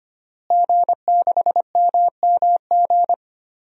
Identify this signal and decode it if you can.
Morse code, heard as a keyed tone, about 25 wpm, 700 Hz. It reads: G6MMG
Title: DXCC-100 Call Signs - 5 Characters - Repeated 3x - 25wpm